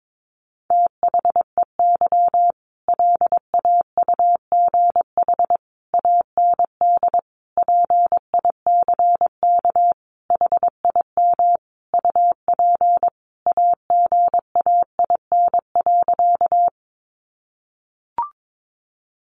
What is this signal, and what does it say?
Morse code, heard as a keyed tone, about 22 wpm, 700 Hz. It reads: THEY LAUGH AND PICK HIM UP AGAIN.
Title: Sentences from Top 1000 Words 22wpm